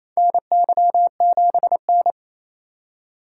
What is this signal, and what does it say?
Morse code, heard as a keyed tone, about 28 wpm, 700 Hz. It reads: NY7N
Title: K1USN SST QSO Practice - 28wpm